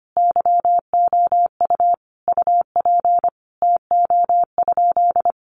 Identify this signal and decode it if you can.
Morse code, heard as a keyed tone, about 25 wpm, 700 Hz. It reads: YOU UP TO?
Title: Sentences from Top 100 Words 25wpm